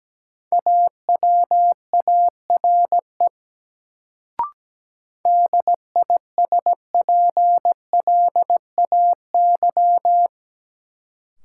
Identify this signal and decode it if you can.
Morse code, heard as a keyed tone, about 17 wpm, 700 Hz. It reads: AWARE DISPLAY
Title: Sentences from Top 1200 Words - New Word Drill Alt - 17wpm